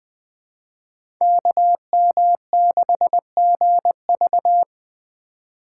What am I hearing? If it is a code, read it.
Morse code, heard as a keyed tone, about 20 wpm, 700 Hz. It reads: KM6GV
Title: USA Call Signs - 5 Characters - 4x Character Spacing - 20wpm